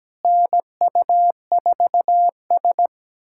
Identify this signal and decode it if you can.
Morse code, heard as a keyed tone, about 17 wpm, 700 Hz. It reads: NU4S
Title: USA Callsign DE Callsign - 4 Characters - 17wpm